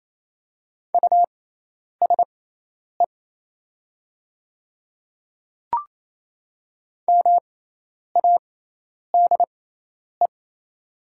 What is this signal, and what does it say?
Morse code, heard as a keyed tone, about 27 wpm, spaced as for 10 wpm, 700 Hz. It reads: USE MADE
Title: 6x Character Spacing - Top 200 Words - 28wpm